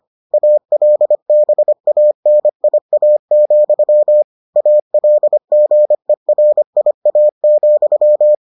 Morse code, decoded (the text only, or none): ALBANIA, ALGERIA,